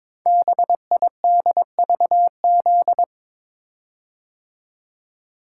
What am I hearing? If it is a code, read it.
Morse code, heard as a keyed tone, about 22 wpm, 700 Hz. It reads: BIDVZ
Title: Code Groups - 5 Letter:Numbers - 22wpm